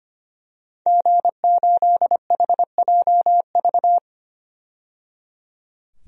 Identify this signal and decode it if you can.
Morse code, heard as a keyed tone, about 25 wpm, 700 Hz. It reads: G8HJV
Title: DXCC-100 Call Signs - 5 Characters - 25wpm